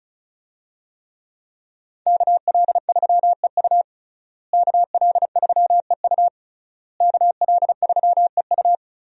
Morse code, heard as a keyed tone, about 34 wpm, 700 Hz. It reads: KL3EU KL3EU KL3EU
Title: USA Call Signs - 5 Characters - Repeated 3x - 35wpm